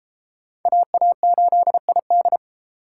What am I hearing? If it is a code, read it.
Morse code, heard as a keyed tone, about 28 wpm, 700 Hz. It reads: AA8ID